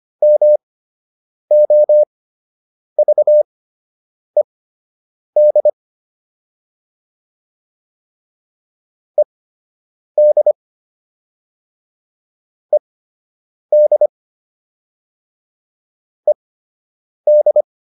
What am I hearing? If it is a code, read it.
Morse code, heard as a keyed tone, about 25 wpm, spaced as for 8 wpm, 600 Hz. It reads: MOVED ED ED ED